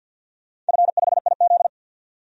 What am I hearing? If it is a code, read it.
Morse code, heard as a keyed tone, about 51 wpm, 700 Hz. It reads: F5IZ
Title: DXCC-100 Call Signs - 4 Characters - 50wpm